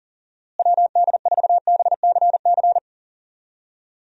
Morse code, heard as a keyed tone, about 38 wpm, 700 Hz. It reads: WD4BCC
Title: USA Call Signs - 6 Characters - Repeated 3x - 40wpm